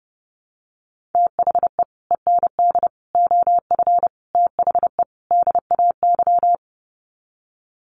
Morse code, heard as a keyed tone, about 30 wpm, 700 Hz. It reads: THE END OF THE DAY
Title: Sets of 5 Words 30wpm